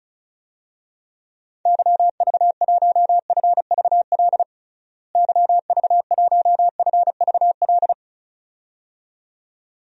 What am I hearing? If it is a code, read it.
Morse code, heard as a keyed tone, about 34 wpm, 700 Hz. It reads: YV1FVL YV1FVL
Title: DXCC-100 Call Signs - 6 Characters - Repeated 2x - 35wpm